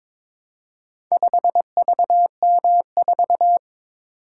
Morse code, heard as a keyed tone, about 22 wpm, 700 Hz. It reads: 5VM4
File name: practice / 22wpm / Four Letter-Number Combinations - 22wpm